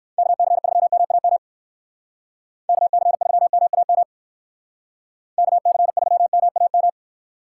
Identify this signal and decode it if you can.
Morse code, heard as a keyed tone, about 58 wpm, 700 Hz. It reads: KC3MAM KC3MAM KC3MAM